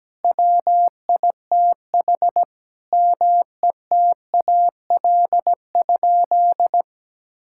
Morse code, heard as a keyed tone, about 17 wpm, 700 Hz. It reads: WITH METAL?
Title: Sentences from Top 1000 Words - Review Part 6 - 17wpm